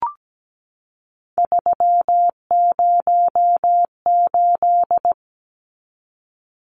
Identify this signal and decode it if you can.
Morse code, heard as a keyed tone, about 17 wpm, 700 Hz. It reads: 308
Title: Numbers - 3-Digit 17wpm